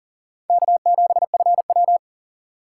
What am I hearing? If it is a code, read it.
Morse code, heard as a keyed tone, about 40 wpm, 700 Hz. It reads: K7FW
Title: USA Call Signs - 4 Characters - 3x Character Spacing - 40wpm